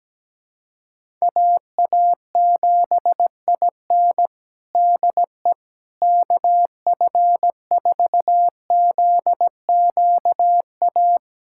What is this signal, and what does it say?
Morse code, heard as a keyed tone, about 17 wpm, 700 Hz. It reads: AA7IN DE KF4ZQA
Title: USA Callsign DE Callsign 17wpm